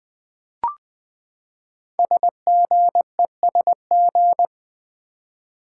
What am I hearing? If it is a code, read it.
Morse code, heard as a keyed tone, about 20 wpm, 700 Hz. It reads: SGESG
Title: Code Groups - 5 Letters - 20wpm